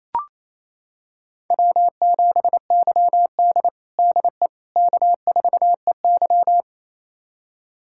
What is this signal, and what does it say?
Morse code, heard as a keyed tone, about 28 wpm, 700 Hz. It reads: W7YD DE K4EY
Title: USA Callsign DE Callsign - 4 Characters - 28wpm